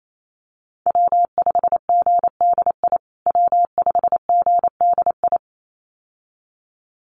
Morse code, heard as a keyed tone, about 28 wpm, 700 Hz. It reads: W5GDI W5GDI
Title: USA Call Signs Repeated 2x 28wpm